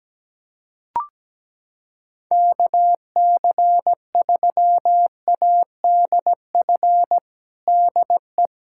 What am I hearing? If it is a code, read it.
Morse code, heard as a keyed tone, about 17 wpm, 700 Hz. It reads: KC3ADF DE
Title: USA Callsign DE Callsign 17wpm